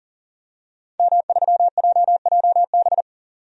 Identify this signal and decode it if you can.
Morse code, heard as a keyed tone, about 40 wpm, 700 Hz. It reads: M3JJB
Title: DXCC-100 Call Signs - 5 Characters - 40wpm